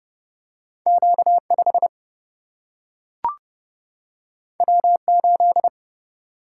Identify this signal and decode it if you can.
Morse code, heard as a keyed tone, about 30 wpm, 700 Hz. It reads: Q5 W8
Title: Single-Letter Single-Number 30wpm